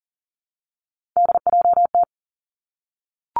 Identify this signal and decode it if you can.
Morse code, heard as a keyed tone, about 38 wpm, 690 Hz. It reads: DJT
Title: Three Challenging Letter-Number Combinations 40wpm